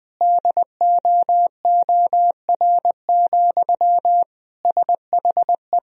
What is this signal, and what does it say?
Morse code, heard as a keyed tone, about 20 wpm, 700 Hz. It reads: DOOR, SHE
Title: Sentences from Top 1000 Words - Review Part 2 - 20wpm